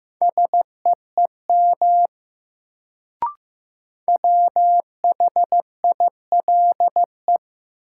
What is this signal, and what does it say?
Morse code, heard as a keyed tone, about 15 wpm, 700 Hz. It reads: SEEM WHILE
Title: Sets of 3 Words - New Word Drill - 15wpm